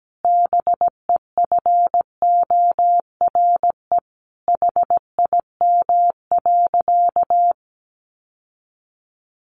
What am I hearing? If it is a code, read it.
Morse code, heard as a keyed tone, about 17 wpm, 700 Hz. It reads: BEFORE HIM.
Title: Sentences from Top 1300 Words 17wpm